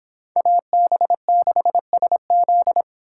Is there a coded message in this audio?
Yes: Morse code from 0.4 to 2.8 s, heard as a keyed tone, about 26 wpm, 700 Hz. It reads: AB6SZ